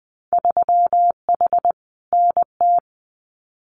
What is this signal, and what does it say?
Morse code, heard as a keyed tone, about 20 wpm, 700 Hz. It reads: 3H NT
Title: Winter Field Day Exchanges Encore 20wpm